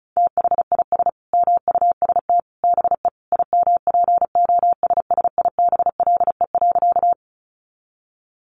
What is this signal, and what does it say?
Morse code, heard as a keyed tone, about 34 wpm, 700 Hz. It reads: THIS MUST BE IMPOSSIBLE.
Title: Sentences from Top 1100 Words 35wpm